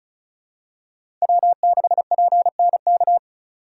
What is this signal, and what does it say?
Morse code, heard as a keyed tone, about 36 wpm, 700 Hz. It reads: W6PNK
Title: USA Call Signs - 5 Characters - 3x Character Spacing - 35wpm